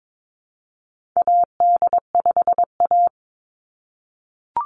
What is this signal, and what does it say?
Morse code, heard as a keyed tone, about 22 wpm, 700 Hz. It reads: AD5A
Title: NA SOTA Chaser Callsigns 22wpm